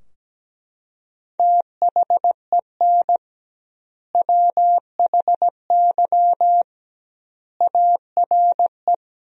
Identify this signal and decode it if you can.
Morse code, heard as a keyed tone, about 17 wpm, 700 Hz. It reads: THEN WHY ARE
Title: 2x Word-Spacing - Sentences from Top 200 Words - 17wpm